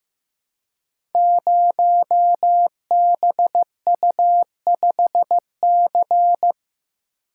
Morse code, heard as a keyed tone, about 15 wpm, 700 Hz. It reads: 0BU5C
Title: Code Groups - 5 Letter:Numbers - 15wpm